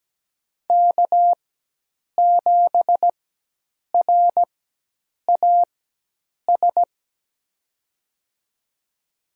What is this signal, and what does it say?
Morse code, heard as a keyed tone, about 17 wpm, spaced as for 8 wpm, 700 Hz. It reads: K7RAS
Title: USA Call Signs - 5 Characters - 4x Character Spacing - 17wpm